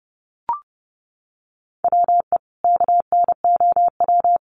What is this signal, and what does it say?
Morse code, heard as a keyed tone, about 30 wpm, 700 Hz. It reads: WE KNOW
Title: Sentences from Top 1100 Words 30wpm